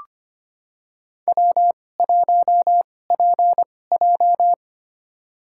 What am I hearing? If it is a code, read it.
Morse code, heard as a keyed tone, about 25 wpm, spaced as for 18 wpm, 700 Hz. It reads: W1PJ
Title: USA Call Signs - 4 Characters - 2x Character Spacing - 25wpm